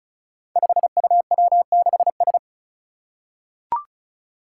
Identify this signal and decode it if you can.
Morse code, heard as a keyed tone, about 35 wpm, 700 Hz. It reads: 5UW6S
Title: Five Challenging Letter-Number Combinations 35wpm